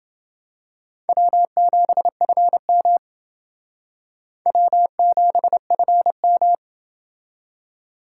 Morse code, heard as a keyed tone, about 28 wpm, 700 Hz. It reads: W7FM W7FM